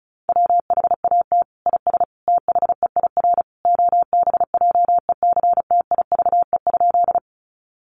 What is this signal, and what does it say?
Morse code, heard as a keyed tone, about 34 wpm, 700 Hz. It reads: WHAT IS THEIR OBJECTIVE?
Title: Sentences from Top 1200 Words 35wpm